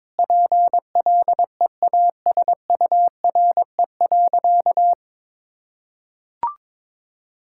Morse code, heard as a keyed tone, about 22 wpm, 700 Hz. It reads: PLEASURE.
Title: Sentences from Top 1600 Words 22wpm